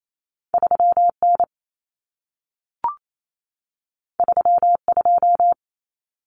Morse code, heard as a keyed tone, about 28 wpm, 700 Hz. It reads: 3N 32
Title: Common Cut Numbers - 2-Digit 28wpm